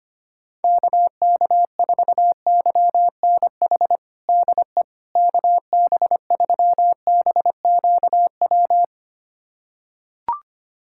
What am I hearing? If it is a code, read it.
Morse code, heard as a keyed tone, about 25 wpm, 700 Hz. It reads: KK4YNH DE KB3BQW
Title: USA Callsign DE Callsign - 6 Characters - 25wpm